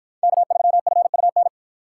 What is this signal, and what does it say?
Morse code, heard as a keyed tone, about 53 wpm, 700 Hz. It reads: K3FUN